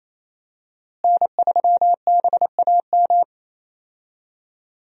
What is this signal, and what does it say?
Morse code, heard as a keyed tone, about 28 wpm, 700 Hz. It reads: N3BAM
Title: USA Call Signs - 5 Characters - Repeated 2x - 28wpm